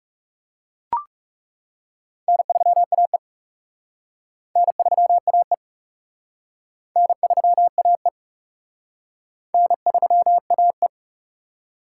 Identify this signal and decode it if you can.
Morse code, heard as a keyed tone, about 40 wpm, 700 Hz. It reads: N3AE N3AE N3AE N3AE